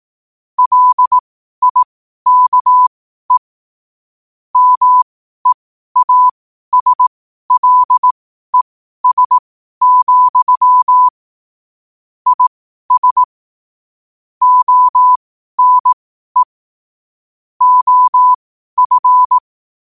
Morse code, heard as a keyed tone, about 18 wpm, spaced as for 12 wpm, 1000 Hz. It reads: LIKE MEASLES, IS ONE OF
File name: qotdfiles / 12wpm